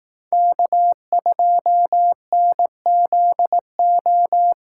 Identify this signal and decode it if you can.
Morse code, heard as a keyed tone, about 18 wpm, 700 Hz. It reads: K2NZO
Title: Speed Racing - USA Call Signs 15wpm